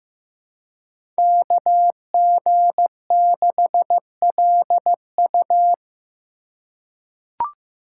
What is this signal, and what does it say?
Morse code, heard as a keyed tone, about 15 wpm, 700 Hz. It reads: KG6LU
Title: USA Call Signs - 4x Character Spacing - 15wpm